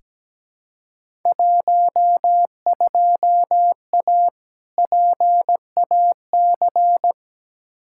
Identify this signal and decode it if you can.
Morse code, heard as a keyed tone, about 17 wpm, 700 Hz. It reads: 12A PAC
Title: Field Day Exchanges Encore - 17wpm